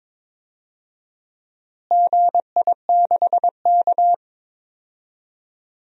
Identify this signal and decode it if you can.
Morse code, heard as a keyed tone, about 22 wpm, 700 Hz. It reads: GI6K
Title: DXCC-100 Call Signs - 4 Characters - Repeated 2x - 22wpm